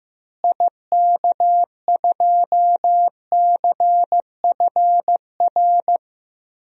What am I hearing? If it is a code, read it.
Morse code, heard as a keyed tone, about 15 wpm, 700 Hz. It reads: IK2CFR
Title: DXCC-100 Call Signs - 6 Characters - Repeated 2x - 15wpm